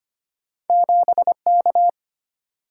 Morse code, heard as a keyed tone, about 25 wpm, 690 Hz. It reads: 7K
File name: Two Letter-Number Combinations - 25wpm